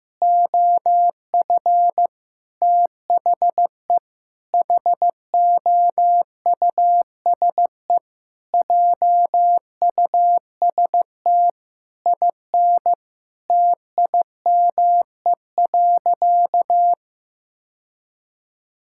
Morse code, heard as a keyed tone, about 16 wpm, 700 Hz. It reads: OF THE HOUSE JUST IN TIME.
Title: Sentences from Top 1600 Words 15wpm